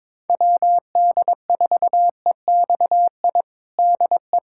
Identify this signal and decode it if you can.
Morse code, heard as a keyed tone, about 22 wpm, 700 Hz. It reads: WD4EXI DE
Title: USA Callsign DE Callsign - 6 Characters - 22wpm